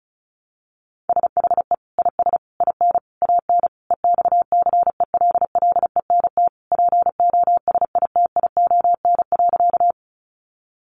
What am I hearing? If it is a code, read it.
Morse code, heard as a keyed tone, about 34 wpm, 700 Hz. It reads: SHE IS IN AN EXCELLENT POSITION.
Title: Sentences from Top 1000 Words 35wpm